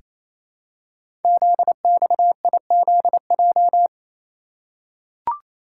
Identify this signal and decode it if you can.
Morse code, heard as a keyed tone, about 29 wpm, 700 Hz. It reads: ZXIZJ
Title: Code Groups - 5 Letters - 28wpm